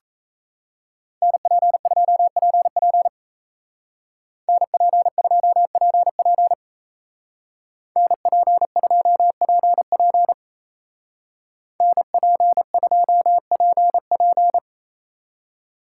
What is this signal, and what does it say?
Morse code, heard as a keyed tone, about 34 wpm, 700 Hz. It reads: NP2PP NP2PP NP2PP NP2PP
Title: Speed Racing - USA 2x2 Call Signs 28wpm